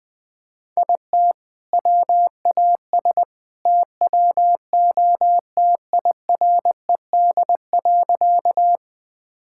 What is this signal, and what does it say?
Morse code, heard as a keyed tone, about 21 wpm, 700 Hz. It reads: IT WAS TWO TIRED.
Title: All-Age Appropriate Jokes - 20wpm